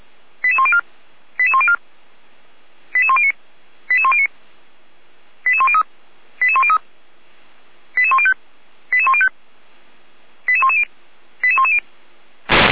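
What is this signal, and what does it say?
Five-tone selective call (ZVEI), heard as quick runs of five tones, tones of about 70 ms. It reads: 88104 88104 88109 88109 88103 88103 88105 88105 88110 88110
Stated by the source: ZVEI five-tone call 88102 received by SDR.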